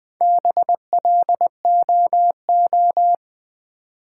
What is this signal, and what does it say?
Morse code, heard as a keyed tone, about 20 wpm, 700 Hz. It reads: BLOO
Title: Build A Word - 1 Letter At A Time - 5-Letter Words - 20wpm